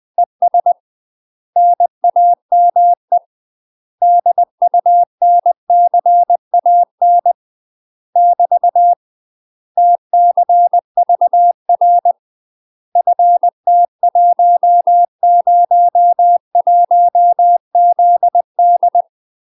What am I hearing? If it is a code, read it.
Morse code, heard as a keyed tone, about 20 wpm, 700 Hz. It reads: ES NAME DUNCAN <BT> TCVR FT101ZD
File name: UK MASTER QSOs TRAINING / 20WPM / UK MASTER QSOs 0028